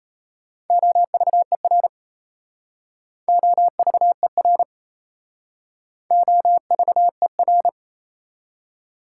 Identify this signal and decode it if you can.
Morse code, heard as a keyed tone, about 36 wpm, 700 Hz. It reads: OVER OVER OVER